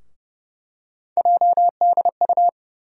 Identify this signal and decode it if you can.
Morse code, heard as a keyed tone, about 30 wpm, 700 Hz. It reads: JDU
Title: USA Call Sign Postfixes - 3 Letters 30wpm